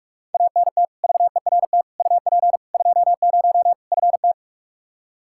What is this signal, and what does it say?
Morse code, heard as a keyed tone, about 46 wpm, 700 Hz. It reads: ANT VERT UP 20 FT
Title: Practice QSO - Exchange 2 - Info - 45wpm